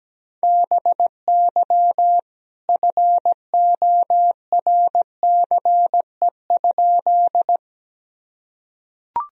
Morse code, heard as a keyed tone, about 17 wpm, 700 Hz. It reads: BY FORCE?
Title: Sentences from Top 1000 Words - Review Part 3 - 17wpm